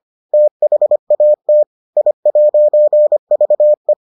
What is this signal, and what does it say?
Morse code, heard as a keyed tone, about 25 wpm, 600 Hz. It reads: THAT I'VE